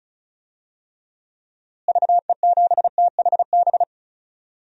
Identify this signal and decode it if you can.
Morse code, heard as a keyed tone, about 36 wpm, 700 Hz. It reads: VE7THB
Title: DXCC-100 Call Signs - 6 Characters - Repeated 2x - 35wpm